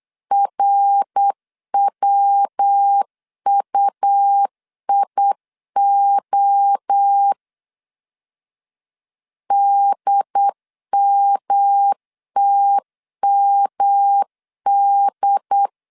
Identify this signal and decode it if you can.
Morse code, heard as a keyed tone, about 8 wpm, 800 Hz. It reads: RWUIO DMTMD
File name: CW / DET / 1 3 1500C S B 8wpm 4141